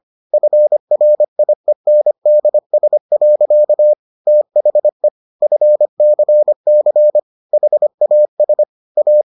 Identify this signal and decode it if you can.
Morse code, heard as a keyed tone, about 25 wpm, 600 Hz. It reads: FRIENDS. THE FCC HAS A